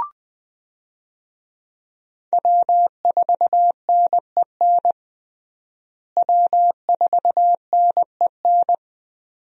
Morse code, heard as a keyed tone, about 20 wpm, 700 Hz. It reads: W4NEN W4NEN
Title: USA Call Signs - 5 Characters - Repeated 2x - 20wpm